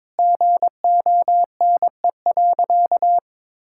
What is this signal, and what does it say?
Morse code, heard as a keyed tone, about 21 wpm, 700 Hz. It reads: GONE.
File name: practice / 22wpm / Sentences with 4-Word Phrases - 22wpm